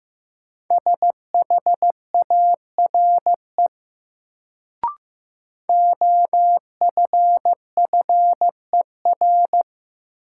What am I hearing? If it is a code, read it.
Morse code, heard as a keyed tone, about 15 wpm, 700 Hz. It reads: SHARE OFFER
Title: Sentences from Top 900 Words - New Word Drill - 15wpm